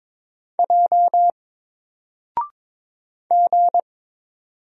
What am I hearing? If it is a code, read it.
Morse code, heard as a keyed tone, about 22 wpm, 700 Hz. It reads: J G
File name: Single-Letter-Number-22wpm